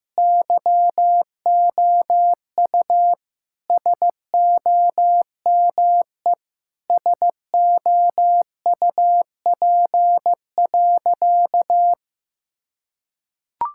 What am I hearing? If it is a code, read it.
Morse code, heard as a keyed tone, about 15 wpm, 700 Hz. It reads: YOU SOME SOUP.
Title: Sentences from Top 1000 Words - Review Part 6 - 15wpm